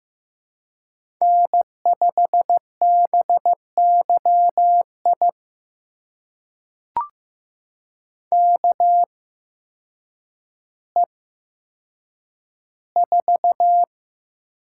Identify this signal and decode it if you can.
Morse code, heard as a keyed tone, about 15 wpm, spaced as for 8 wpm, 700 Hz. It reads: N5BYI KE4
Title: USA Call Signs - 8x Character Spacing - 15wpm